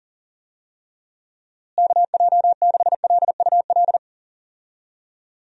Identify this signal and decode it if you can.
Morse code, heard as a keyed tone, about 40 wpm, 700 Hz. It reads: KJ6LUL
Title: USA Call Signs - 6 Characters - Repeated 2x - 40wpm